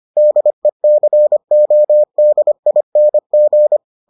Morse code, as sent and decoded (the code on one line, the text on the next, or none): -.. -.-. --- -.. .. -. --.
DECODING